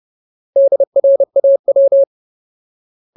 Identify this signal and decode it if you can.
Morse code, heard as a keyed tone, about 30 wpm, 550 Hz. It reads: DRAW
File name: Ultimate Word List / 30WPM Ultimate